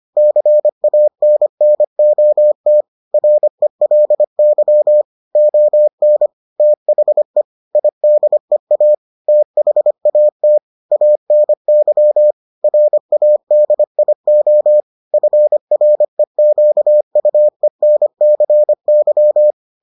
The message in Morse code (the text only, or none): CANNOT RELY ON THE IDEA THAT ANY RADIO FREQUENCY